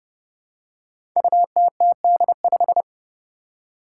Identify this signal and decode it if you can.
Morse code, heard as a keyed tone, about 31 wpm, 700 Hz. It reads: UTTD5
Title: Five Challenging Letter-Number Combinations 30wpm